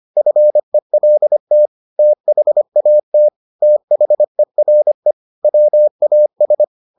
Morse code, heard as a keyed tone, about 25 wpm, 600 Hz. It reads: FELT THAT THERE WAS